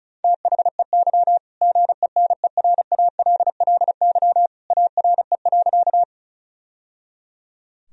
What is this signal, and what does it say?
Morse code, heard as a keyed tone, about 34 wpm, 700 Hz. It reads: THEY GENERALLY ARE.
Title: Sentences from Top 1000 Words - Review Part 4 - 35wpm